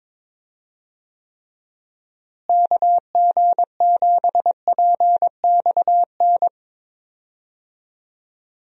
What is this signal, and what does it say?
Morse code, heard as a keyed tone, about 22 wpm, 700 Hz. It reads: KG7PXN
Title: USA Call Signs - 6 Characters - Repeated 2x - 22wpm